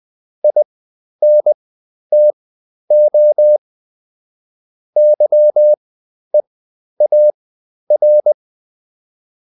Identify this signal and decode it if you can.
Morse code, heard as a keyed tone, about 20 wpm, spaced as for 11 wpm, 600 Hz. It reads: INTO YEAR
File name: Top-100-Words-10